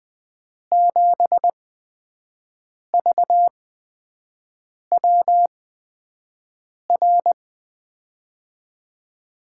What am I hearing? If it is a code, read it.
Morse code, heard as a keyed tone, about 21 wpm, spaced as for 5 wpm, 700 Hz. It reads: 7VWR
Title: USA Call Signs - 6 Characters - 8x Character Spacing - 20wpm